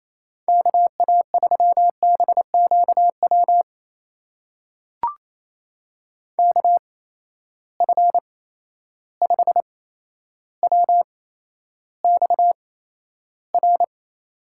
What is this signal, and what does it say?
Morse code, heard as a keyed tone, about 28 wpm, spaced as for 12 wpm, 700 Hz. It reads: KA3BQW KF5WXR
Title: USA Call Signs - 6 Characters - 8x Character Spacing - 28wpm